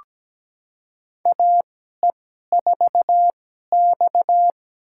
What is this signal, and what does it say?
Morse code, heard as a keyed tone, about 17 wpm, spaced as for 12 wpm, 700 Hz. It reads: AE4X
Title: USA Call Signs - 4 Characters - 2x Character Spacing - 17wpm